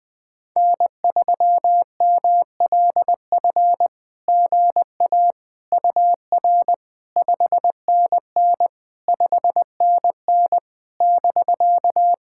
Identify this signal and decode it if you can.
Morse code, heard as a keyed tone, about 20 wpm, 700 Hz. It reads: N3MLF GA UR 5NN 5NN <BK>
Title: SOTA QSO Practice 20wpm